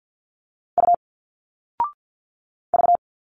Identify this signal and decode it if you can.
Morse code, heard as a keyed tone, about 50 wpm, 700 Hz. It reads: U V
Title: Single Letters 50wpm